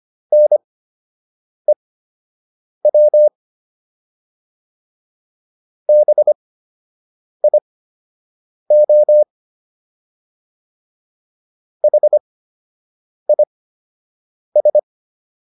Morse code, heard as a keyed tone, about 25 wpm, spaced as for 7 wpm, 600 Hz. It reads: NEW BIO HIS